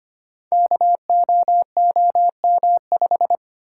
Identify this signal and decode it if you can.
Morse code, heard as a keyed tone, about 25 wpm, 700 Hz. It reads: KOOM5
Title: Code Groups - 5 Letter:Numbers - 25wpm